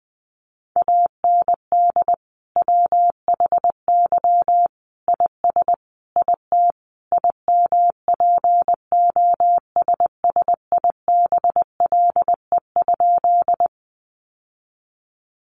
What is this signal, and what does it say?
Morse code, heard as a keyed tone, about 20 wpm, 700 Hz. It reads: AND WHY IS IT IMPOSSIBLE?
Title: Sentences from Top 1100 Words 20wpm